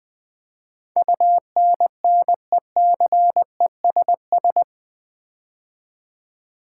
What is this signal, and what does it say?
Morse code, heard as a keyed tone, about 20 wpm, 700 Hz. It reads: UNNECESS
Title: Build A Word - 4 Letters At A Time - 11-Letter Words - 20wpm